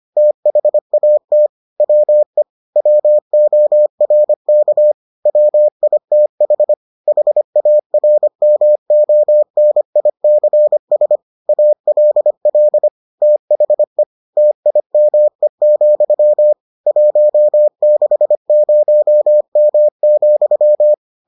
Morse code, heard as a keyed tone, about 25 wpm, 600 Hz. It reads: THAT WE WORK WITH HARMONICS ALL THE TIME, 160M,